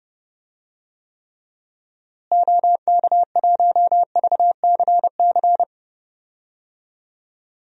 Morse code, heard as a keyed tone, about 32 wpm, 700 Hz. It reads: OK1VCC
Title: DXCC-100 Call Signs - 6 Characters - Repeated 2x - 30wpm